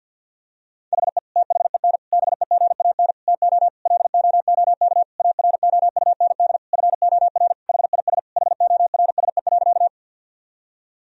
Morse code, heard as a keyed tone, about 50 wpm, 700 Hz. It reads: HE THEN BEGAN TO LOOK AROUND FOR HIS HORSE.
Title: Sentences from Top 1000 Words - Review Part 3 - 50wpm